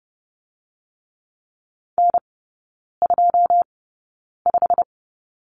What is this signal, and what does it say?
Morse code, heard as a keyed tone, about 30 wpm, spaced as for 9 wpm, 700 Hz. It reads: N25
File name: course / 30:10wpm / 087 Lesson 15e - Number 5 - Three Characters 30wpm